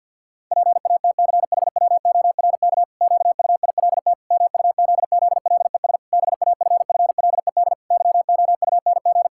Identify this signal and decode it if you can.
Morse code, heard as a keyed tone, about 50 wpm, 700 Hz. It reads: PATCHWORK QUILT MUZZLES BAFFLED YOUNG
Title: Pangram Challenge v2 - 50wpm